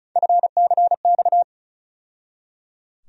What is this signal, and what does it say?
Morse code, heard as a keyed tone, about 36 wpm, 700 Hz. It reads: FCX
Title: USA Call Sign Postfixes - 3 Letters 35wpm